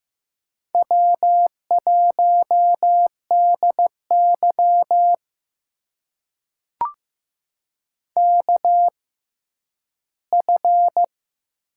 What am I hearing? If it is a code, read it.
Morse code, heard as a keyed tone, about 15 wpm, spaced as for 10 wpm, 700 Hz. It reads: W1DY KF
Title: USA Call Signs - 6x Character Spacing - 15wpm